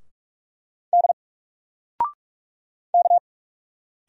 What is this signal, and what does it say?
Morse code, heard as a keyed tone, about 44 wpm, 700 Hz. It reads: D K
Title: Single Letter-Number 45wpm